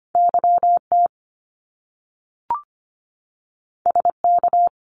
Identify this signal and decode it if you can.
Morse code, heard as a keyed tone, about 25 wpm, 700 Hz. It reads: YT SK
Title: Canadian Provinces and Territories 25wpm